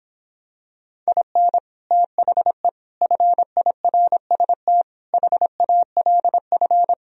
Morse code, heard as a keyed tone, about 26 wpm, 700 Hz. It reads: IN THE FIRST HALF